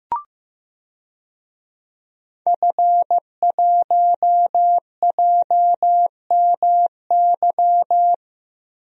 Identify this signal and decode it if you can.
Morse code, heard as a keyed tone, about 16 wpm, 700 Hz. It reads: F1JMY
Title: DXCC-100 Call Signs - 5 Characters - Repeated 2x - 15wpm